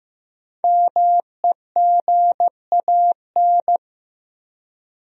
Morse code, heard as a keyed tone, about 15 wpm, 700 Hz. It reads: MEGAN
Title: USA Common Names - Max 5 Letters - 15wpm